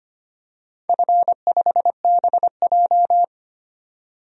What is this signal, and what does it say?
Morse code, heard as a keyed tone, about 25 wpm, 700 Hz. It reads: F5BJ
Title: DXCC-100 Call Signs - 4 Characters - 25wpm